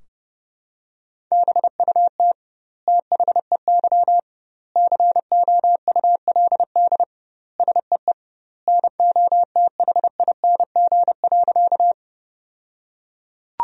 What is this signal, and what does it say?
Morse code, heard as a keyed tone, about 30 wpm, 700 Hz. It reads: BUT THEY COULD SEE NOTHING.
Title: 2x Word-Spacing - Sentences from Top 500 Words - 30wpm